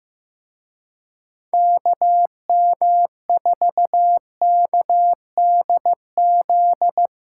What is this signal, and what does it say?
Morse code, heard as a keyed tone, about 17 wpm, 700 Hz. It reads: KM4KDZ
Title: Speed Racing - USA 2x3 Call Signs 15wpm